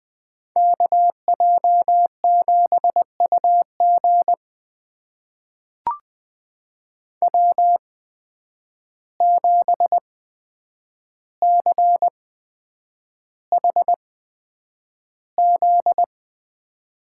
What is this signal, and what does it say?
Morse code, heard as a keyed tone, about 20 wpm, spaced as for 9 wpm, 700 Hz. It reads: KJ7UG W7CHZ
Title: USA Call Signs - 5 Characters - 8x Character Spacing - 20wpm